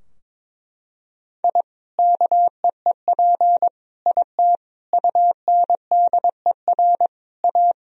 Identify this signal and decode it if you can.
Morse code, heard as a keyed tone, about 22 wpm, 700 Hz. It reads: I KEEP IT UNDER A
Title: Sentences from Top 1000 Words - Review Part 2 - 22wpm